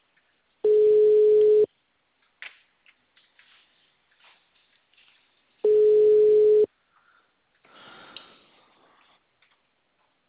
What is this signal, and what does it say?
Ringback tone (Europe, 425 Hz).